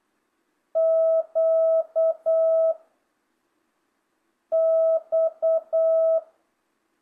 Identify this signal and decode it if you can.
Morse code, heard as a keyed tone, about 8 wpm, 660 Hz. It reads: Q X
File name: jh8cbh / 2025cw